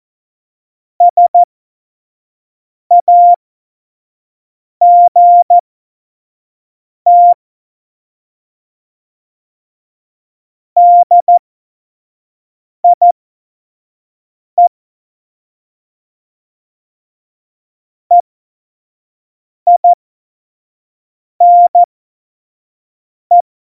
Morse code, heard as a keyed tone, about 14 wpm, spaced as for 5 wpm, 700 Hz. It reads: SAGT DIE EINE
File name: Witz 31. Oktober 17, Tempo 25